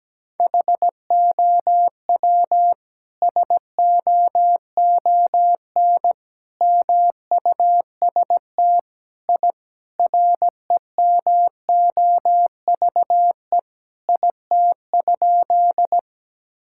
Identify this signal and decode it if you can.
Morse code, heard as a keyed tone, about 17 wpm, 700 Hz. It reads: HOW SOON MUST I REMOVE IT?
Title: Sentences from Top 1000 Words - Review Part 5 - 17wpm